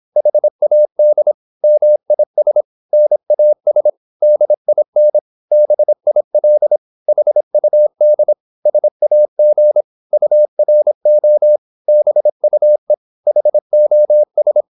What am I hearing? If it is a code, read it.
Morse code, heard as a keyed tone, about 26 wpm, 600 Hz. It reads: HAD MIS NAS DIN BIL HUD SAG URO BUE HOS
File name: Korte ord DK 26wpm